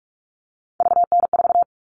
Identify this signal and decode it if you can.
Morse code, heard as a keyed tone, about 44 wpm, 700 Hz. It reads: VN4
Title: All Cut Numbers - 3-Digit 45wpm